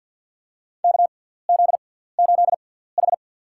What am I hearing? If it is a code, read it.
Morse code, heard as a keyed tone, about 53 wpm, spaced as for 17 wpm, 700 Hz. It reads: KZ8H
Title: USA Call Signs - 6x Character Spacing - 50wpm